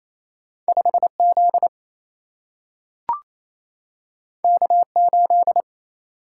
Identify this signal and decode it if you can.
Morse code, heard as a keyed tone, about 28 wpm, 700 Hz. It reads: 5Z K8
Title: Single-Letter Single-Number 28wpm